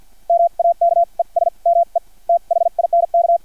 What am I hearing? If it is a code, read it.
Morse code, heard as a keyed tone, about 50 wpm, spaced as for 40 wpm, 680 Hz, with radio noise. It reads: MAKES ME THINK
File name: archived / CRSnet